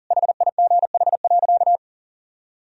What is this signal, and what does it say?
Morse code, heard as a keyed tone, about 40 wpm, 700 Hz. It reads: HIGH.
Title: Sentences from Top 1000 Words - Review Part 1 - 40wpm